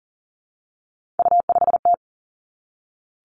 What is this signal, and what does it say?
Morse code, heard as a keyed tone, about 40 wpm, 700 Hz. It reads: U5T
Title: Three Challenging Letter-Number Combinations 40wpm